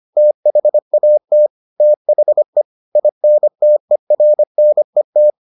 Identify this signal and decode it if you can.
Morse code, heard as a keyed tone, about 25 wpm, 600 Hz. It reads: THAT THE INTERNET